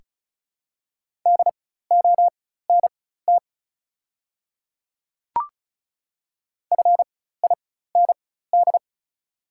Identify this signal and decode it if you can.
Morse code, heard as a keyed tone, about 36 wpm, spaced as for 16 wpm, 700 Hz. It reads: DONT FIND
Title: 4x Character Spacing - Top 300 Words - 35wpm